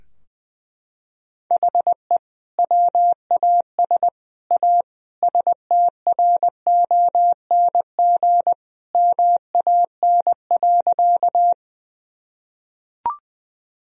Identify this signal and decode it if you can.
Morse code, heard as a keyed tone, about 21 wpm, 700 Hz. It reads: HE WAS A STRONG MAN.
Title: Sentences from Top 1000 Words - Review Part 3 - 20wpm